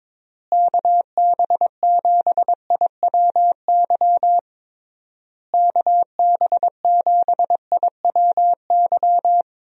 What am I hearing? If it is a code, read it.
Morse code, heard as a keyed tone, about 22 wpm, 700 Hz. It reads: KB7IWY KB7IWY
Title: USA Call Signs - 6 Characters - Repeated 3x - 22wpm